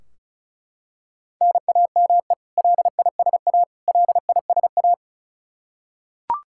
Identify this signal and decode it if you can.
Morse code, heard as a keyed tone, about 36 wpm, 700 Hz. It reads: NAME LISA LISA
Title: Practice QSO - Exchange 1 - Beginning - 35wpm